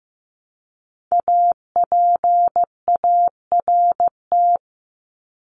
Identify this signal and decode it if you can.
Morse code, heard as a keyed tone, about 15 wpm, 700 Hz. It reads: APART
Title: Top 1501-1750 Words 15wpm